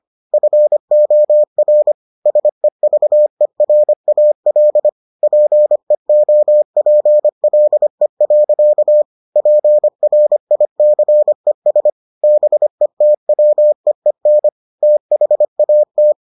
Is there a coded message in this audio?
Yes: Morse code from 0.3 to 16.1 s, heard as a keyed tone, about 25 wpm, 600 Hz. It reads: FOR SEVERAL PEOPLE. PRICES BETWEEN THAT